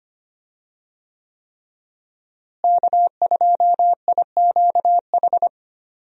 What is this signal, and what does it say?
Morse code, heard as a keyed tone, about 25 wpm, 700 Hz. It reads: K2IQH
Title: USA Call Signs - 5 Characters - Repeated 3x - 25wpm